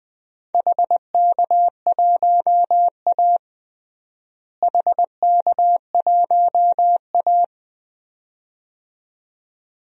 Morse code, heard as a keyed tone, about 20 wpm, 700 Hz. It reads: HK1A HK1A
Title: DXCC-100 Call Signs - 4 Characters - Repeated 2x - 20wpm